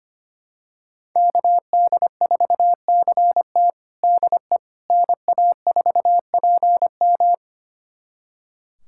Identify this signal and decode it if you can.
Morse code, heard as a keyed tone, about 25 wpm, 700 Hz. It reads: KD4CT DE NA4PM
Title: USA Callsign DE Callsign - 5 Characters - 25wpm